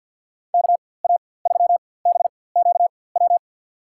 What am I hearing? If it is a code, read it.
Morse code, heard as a keyed tone, about 49 wpm, spaced as for 23 wpm, 700 Hz. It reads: KA3BQW
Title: USA Call Signs - 6 Characters - 4x Character Spacing - 50wpm